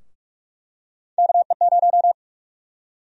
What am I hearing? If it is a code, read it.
Morse code, heard as a keyed tone, about 45 wpm, 700 Hz. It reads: KE0
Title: USA Call Sign Prefixes - 45wpm